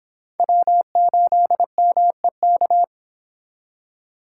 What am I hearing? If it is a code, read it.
Morse code, heard as a keyed tone, about 26 wpm, 700 Hz. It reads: W8MEK